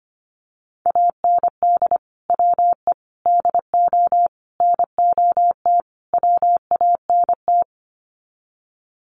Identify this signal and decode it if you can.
Morse code, heard as a keyed tone, about 25 wpm, 700 Hz. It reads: AND WE DO NOT WANT
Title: Sets of 5 Words Encore2 25wpm